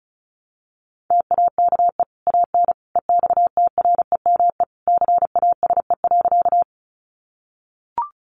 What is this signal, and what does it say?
Morse code, heard as a keyed tone, about 34 wpm, 700 Hz. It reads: TAKE AN EXTREME CASE.
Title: Sentences from Top 1300 Words 35wpm